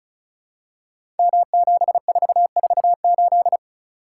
Morse code, heard as a keyed tone, about 36 wpm, 700 Hz. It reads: M7448
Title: Code Groups - 5 Letter:Numbers - 35wpm